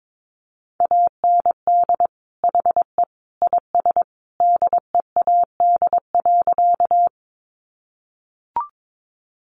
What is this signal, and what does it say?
Morse code, heard as a keyed tone, about 22 wpm, 700 Hz. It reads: AND HE IS DEAD.
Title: Sentences from Top 1500 Words 22wpm